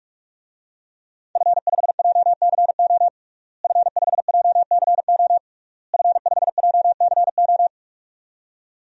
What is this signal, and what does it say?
Morse code, heard as a keyed tone, about 46 wpm, 700 Hz. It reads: F5JCO F5JCO F5JCO
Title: DXCC-100 Call Signs - 5 Characters - Repeated 3x - 45wpm